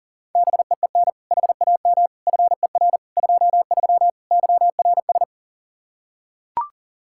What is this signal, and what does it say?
Morse code, heard as a keyed tone, about 40 wpm, 700 Hz. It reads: BEEN HAM FER 23 YRS
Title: Practice QSO - Exchange 3 - More Info - 40wpm